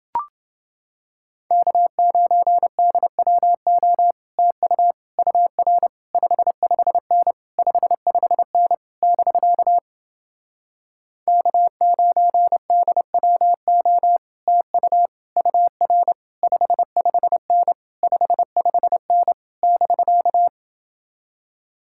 Morse code, heard as a keyed tone, about 29 wpm, 700 Hz. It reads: K9DWO TU UR 55N 55N <BK> K9DWO TU UR 55N 55N <BK>